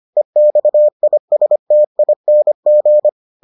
Morse code, heard as a keyed tone, about 25 wpm, 600 Hz. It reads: EXISTING